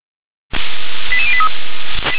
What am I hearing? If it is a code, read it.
Five-tone selective call (ZVEI), heard as quick runs of five tones, tones of about 70 ms. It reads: 88083
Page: SDR/ZVEI